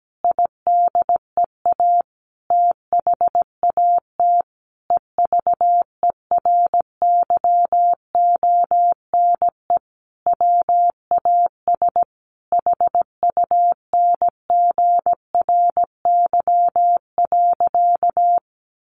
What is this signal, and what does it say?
Morse code, heard as a keyed tone, about 17 wpm, 700 Hz. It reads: IDEA THAT EVERYONE WAS HUNGRY.
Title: Sentences from Top 1400 Words 17wpm